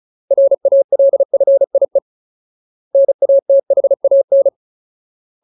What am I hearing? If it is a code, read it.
Morse code, heard as a keyed tone, about 34 wpm, 550 Hz. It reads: RALFIE NATHAN
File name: Ultimate Word List / 35WPM Ultimate